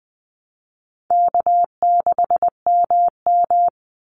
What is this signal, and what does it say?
Morse code, heard as a keyed tone, about 20 wpm, 700 Hz. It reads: K6MM
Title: Call Signs - 20wpm